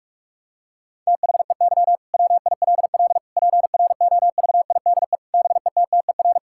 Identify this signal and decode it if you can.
Morse code, heard as a keyed tone, about 41 wpm, 700 Hz. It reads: THEY WILL PROVIDE BETTER
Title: Sentences from Top 1000 Words - Review Part 5 - 45wpm